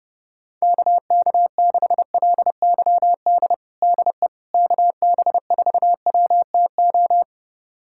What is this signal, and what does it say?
Morse code, heard as a keyed tone, about 30 wpm, 700 Hz. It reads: KK6LYD DE KB4WTO
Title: USA Callsign DE Callsign - 6 Characters - 30wpm